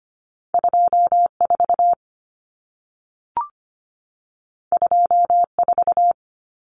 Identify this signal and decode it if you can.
Morse code, heard as a keyed tone, about 25 wpm, 700 Hz. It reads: 24 24
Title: Numbers - 2-Digit 25wpm